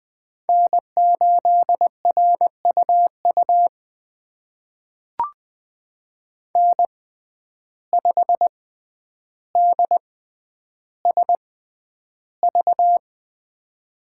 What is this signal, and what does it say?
Morse code, heard as a keyed tone, about 20 wpm, spaced as for 10 wpm, 700 Hz. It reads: N8RUU N5DSV
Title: USA Call Signs - 6x Character Spacing - 20wpm